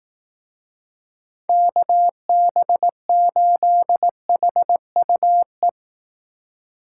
Morse code, heard as a keyed tone, about 18 wpm, 700 Hz. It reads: KB8HUE
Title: Speed Racing - USA 2x3 Call Signs 15wpm